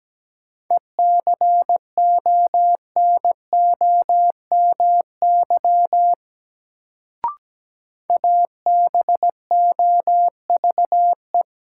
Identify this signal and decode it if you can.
Morse code, heard as a keyed tone, about 17 wpm, 700 Hz. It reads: ECONOMY ABOVE
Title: Sentences from Top 1200 Words - New Word Drill Alt - 17wpm